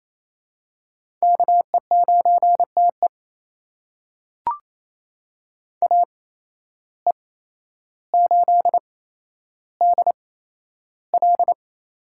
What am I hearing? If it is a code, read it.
Morse code, heard as a keyed tone, about 28 wpm, spaced as for 12 wpm, 700 Hz. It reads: KE9TE AE8DL
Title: USA Call Signs - 5 Characters - 8x Character Spacing - 28wpm